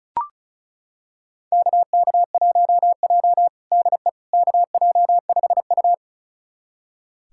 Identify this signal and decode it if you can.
Morse code, heard as a keyed tone, about 36 wpm, 700 Hz. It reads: KK1J DE KJ5U
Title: USA Callsign DE Callsign - 4 Characters - 35wpm